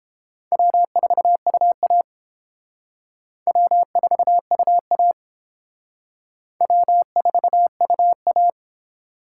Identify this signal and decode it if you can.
Morse code, heard as a keyed tone, about 31 wpm, 700 Hz. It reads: W4UA W4UA W4UA